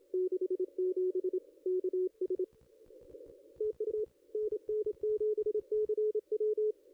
Morse code, heard as a keyed tone, about 27 wpm, 430 Hz, with radio noise. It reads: 67 KS TU NN7CW
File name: contests / ss-cw-k5tr-op-n5rz -0500